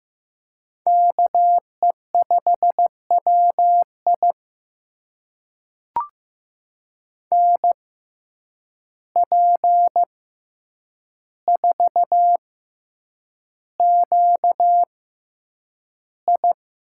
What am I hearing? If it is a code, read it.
Morse code, heard as a keyed tone, about 15 wpm, spaced as for 8 wpm, 700 Hz. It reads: KE5WI NP4QI
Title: USA Call Signs - 5 Characters - 6x Character Spacing - 15wpm